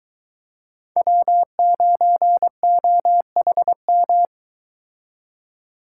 Morse code, heard as a keyed tone, about 23 wpm, 700 Hz. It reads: W9OHM